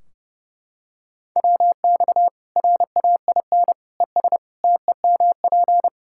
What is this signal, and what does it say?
Morse code, heard as a keyed tone, about 30 wpm, 700 Hz. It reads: WX RAIN ES TEMP
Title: Practice QSO - Exchange 2 - Info - 30wpm